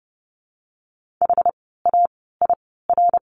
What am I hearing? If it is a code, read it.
Morse code, heard as a keyed tone, about 30 wpm, spaced as for 17 wpm, 700 Hz. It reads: HAIR
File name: Lesson 17f - Letter u - Words 30wpm